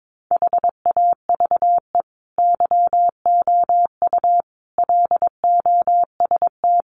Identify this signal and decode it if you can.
Morse code, heard as a keyed tone, about 22 wpm, 700 Hz. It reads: HAVE YOU LOST
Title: Sentences from Top 1400 Words 22wpm